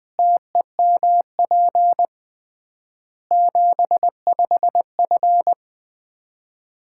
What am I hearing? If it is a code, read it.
Morse code, heard as a keyed tone, about 21 wpm, 700 Hz. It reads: TEMP 75F
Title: Common QSO Phrases - 3x Word-Spacing - 20wpm